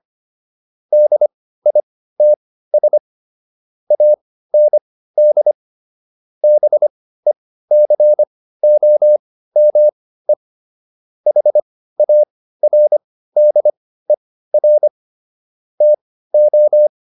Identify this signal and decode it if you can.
Morse code, heard as a keyed tone, about 25 wpm, spaced as for 15 wpm, 600 Hz. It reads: DITS AND BECOME HARDER TO